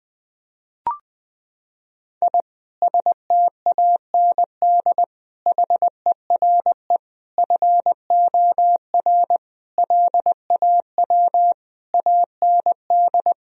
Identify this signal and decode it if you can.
Morse code, heard as a keyed tone, about 21 wpm, 700 Hz. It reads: I STAND HERE FOR LAW AND
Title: Sentences from Top 1000 Words - Review Part 3 - 20wpm